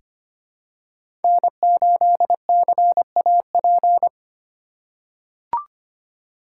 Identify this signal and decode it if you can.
Morse code, heard as a keyed tone, about 25 wpm, 700 Hz. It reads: N8CAP
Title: USA Call Signs - 3x Character Spacing - 25wpm